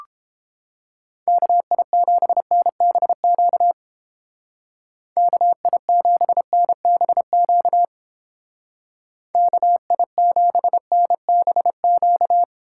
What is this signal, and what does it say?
Morse code, heard as a keyed tone, about 29 wpm, 700 Hz. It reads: KI7NBQ KI7NBQ KI7NBQ